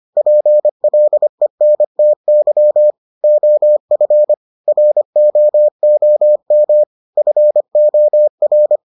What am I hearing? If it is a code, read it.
Morse code, heard as a keyed tone, about 25 wpm, 600 Hz. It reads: PLENTY OF ROOM FOR